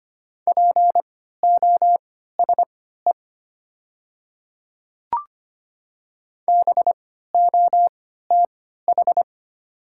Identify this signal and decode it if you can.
Morse code, heard as a keyed tone, about 25 wpm, spaced as for 14 wpm, 700 Hz. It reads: POSE BOTH
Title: 3x Character Spacing - Top 500 Words - 25wpm